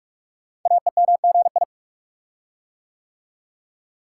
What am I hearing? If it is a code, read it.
Morse code, heard as a keyed tone, about 44 wpm, 700 Hz. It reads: AEMGI
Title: Code Groups - 5 Letters - 45wpm